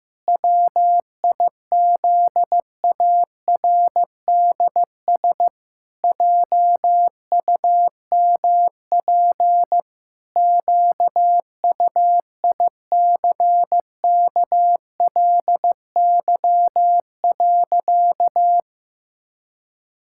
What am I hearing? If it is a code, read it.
Morse code, heard as a keyed tone, about 15 wpm, 700 Hz. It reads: WIZARDS JUMP QUICKLY.
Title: Pangram Challenge v2 - 15wpm